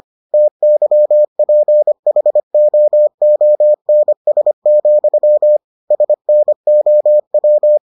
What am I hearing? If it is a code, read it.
Morse code, heard as a keyed tone, about 25 wpm, 600 Hz. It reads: TYPHOONS, SNOW